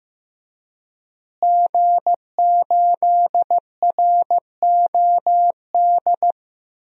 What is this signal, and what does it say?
Morse code, heard as a keyed tone, about 15 wpm, 700 Hz. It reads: G8ROD
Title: DXCC-100 Call Signs - 5 Characters - Repeated 3x - 15wpm